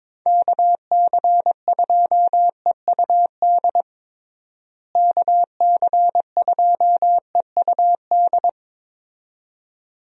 Morse code, heard as a keyed tone, about 22 wpm, 700 Hz. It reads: KC2EUD KC2EUD
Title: USA Call Signs - 6 Characters - Repeated 2x - 22wpm